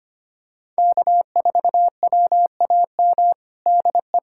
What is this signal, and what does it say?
Morse code, heard as a keyed tone, about 25 wpm, 700 Hz. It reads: K4WAM DE
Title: USA Callsign DE Callsign - 5 Characters - 25wpm